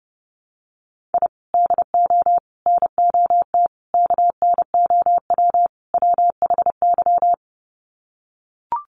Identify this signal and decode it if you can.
Morse code, heard as a keyed tone, about 30 wpm, 700 Hz. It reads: I DO NOT KNOW WHY
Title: Sets of 5 Words 30wpm